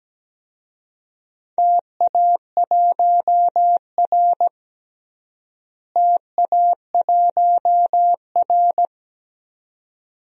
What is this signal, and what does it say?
Morse code, heard as a keyed tone, about 17 wpm, 700 Hz. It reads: TA1R TA1R
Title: DXCC-100 Call Signs - 4 Characters - Repeated 2x - 17wpm